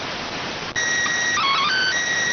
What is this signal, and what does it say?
SSTV (slow-scan television), PD 290.